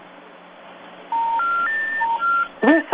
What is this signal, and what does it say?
Telephone call progress tone, heard as a telephone network's tone. Special information tone.